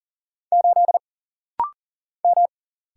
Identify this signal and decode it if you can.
Morse code, heard as a keyed tone, about 40 wpm, 700 Hz. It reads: 8 M
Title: Single Letter-Number Alt - Rapid-Fire 40wpm